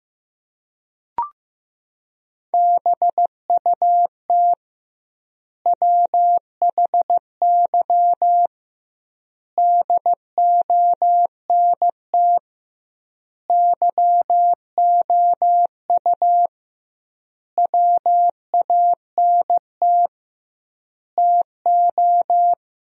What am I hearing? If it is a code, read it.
Morse code, heard as a keyed tone, about 16 wpm, 700 Hz. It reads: BUT WHY DONT YOU WANT TO
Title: 2x Word-Spacing - Sentences from Top 300 Words - 15wpm